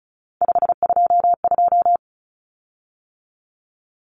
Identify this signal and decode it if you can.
Morse code, heard as a keyed tone, about 34 wpm, 700 Hz. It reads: 522
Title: Common Cut Numbers - 3-Digit 35wpm